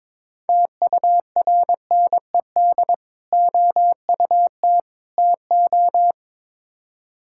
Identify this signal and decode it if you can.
Morse code, heard as a keyed tone, about 21 wpm, 700 Hz. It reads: TURNED OUT TO
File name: practice / 22wpm / Build A Phrase - 5-Word Phrases - 22wpm